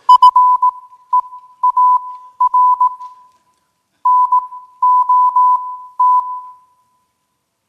Morse code, heard as a keyed tone, about 18 wpm, spaced as for 12 wpm, 1000 Hz. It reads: FEAR NOT